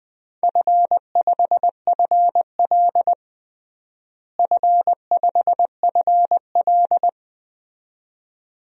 Morse code, heard as a keyed tone, about 20 wpm, 700 Hz. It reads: F5FL F5FL
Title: DXCC-100 Call Signs - 4 Characters - Repeated 2x - 20wpm